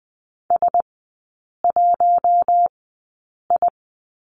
Morse code, heard as a keyed tone, about 20 wpm, spaced as for 8 wpm, 700 Hz. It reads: S1I
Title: 041 Lesson 8e - Number 1 - Three Characters 20wpm